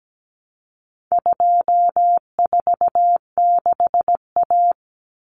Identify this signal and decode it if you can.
Morse code, heard as a keyed tone, about 17 wpm, 700 Hz. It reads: 246A
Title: Common Cut Numbers - 4-Digit 17wpm